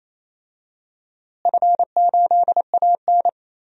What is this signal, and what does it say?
Morse code, heard as a keyed tone, about 28 wpm, 700 Hz. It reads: F8AN
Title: DXCC-100 Call Signs - 4 Characters - Repeated 2x - 28wpm